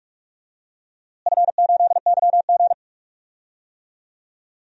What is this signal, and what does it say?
Morse code, heard as a keyed tone, about 42 wpm, 700 Hz. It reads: F8YG
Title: DXCC-100 Call Signs - 4 Characters - Repeated 3x - 45wpm